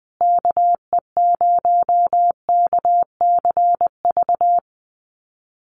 Morse code, heard as a keyed tone, about 20 wpm, 700 Hz. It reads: KE0KCV
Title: USA Call Signs Repeated 3x 20wpm